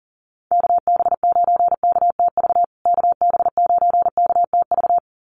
Morse code, heard as a keyed tone, about 40 wpm, 700 Hz. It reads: KB9KTV KB9KTV
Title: USA Call Signs Repeated 2x 40wpm